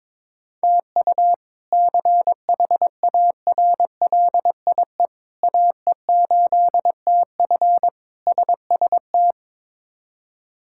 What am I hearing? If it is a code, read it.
Morse code, heard as a keyed tone, about 22 wpm, 700 Hz. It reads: TU CHARLIE AE8TF SST
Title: K1USN SST QSO Practice - 22wpm